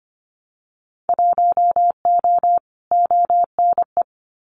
Morse code, heard as a keyed tone, about 25 wpm, 700 Hz. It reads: 1O ONE
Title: Winter Field Day Exchanges 25wpm